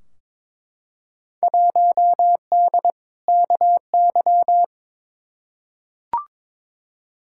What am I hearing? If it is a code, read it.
Morse code, heard as a keyed tone, about 22 wpm, 700 Hz. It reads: 1D KY
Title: Field Day Exchanges - 22wpm